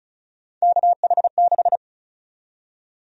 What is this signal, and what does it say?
Morse code, heard as a keyed tone, about 36 wpm, 700 Hz. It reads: KH6
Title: USA Call Sign Prefixes - 35wpm